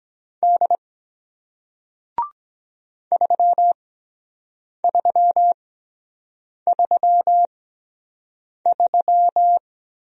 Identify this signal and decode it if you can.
Morse code, heard as a keyed tone, about 23 wpm, spaced as for 7 wpm, 700 Hz. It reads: D 3333